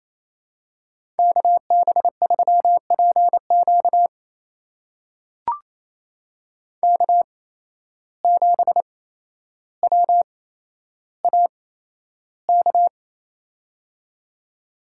Morse code, heard as a keyed tone, about 28 wpm, spaced as for 12 wpm, 700 Hz. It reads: KB3PQ K7WAK
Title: USA Call Signs - 5 Characters - 8x Character Spacing - 28wpm